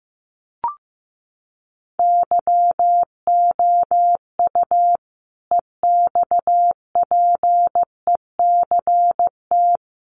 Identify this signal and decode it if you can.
Morse code, heard as a keyed tone, about 15 wpm, 700 Hz. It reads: YOU EXPECT
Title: Sentences from Top 1100 Words 15wpm